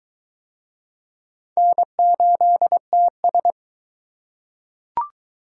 Morse code, heard as a keyed tone, about 21 wpm, 700 Hz. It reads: N8TS